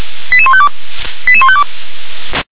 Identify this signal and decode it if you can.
Five-tone selective call (ZVEI), heard as quick runs of five tones, tones of about 70 ms. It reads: 88152 88152